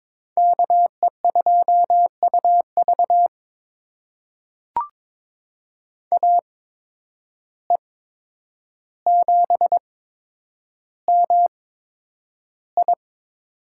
Morse code, heard as a keyed tone, about 21 wpm, spaced as for 9 wpm, 700 Hz. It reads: KE2UV AE7MI
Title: USA Call Signs - 5 Characters - 8x Character Spacing - 22wpm